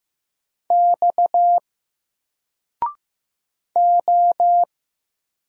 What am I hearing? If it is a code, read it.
Morse code, heard as a keyed tone, about 15 wpm, 700 Hz. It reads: X O
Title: Single Letter-Number Alt 15wpm